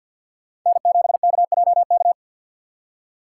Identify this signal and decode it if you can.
Morse code, heard as a keyed tone, about 47 wpm, 690 Hz. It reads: N7KJK
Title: USA Call Signs - 6x Character Spacing - 50wpm